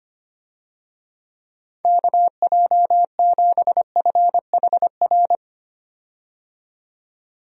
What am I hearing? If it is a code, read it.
Morse code, heard as a keyed tone, about 25 wpm, 700 Hz. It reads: KJ7FHR
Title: USA Call Signs - 6 Characters - 25wpm